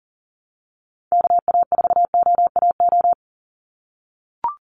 Morse code, heard as a keyed tone, about 40 wpm, 700 Hz. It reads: KA4OAO
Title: USA Call Signs 40wpm